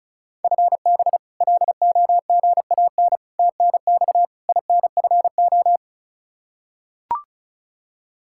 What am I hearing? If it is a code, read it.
Morse code, heard as a keyed tone, about 34 wpm, 700 Hz. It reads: FB LOGAN TNX INFO
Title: K1USN SST QSO Practice 35wpm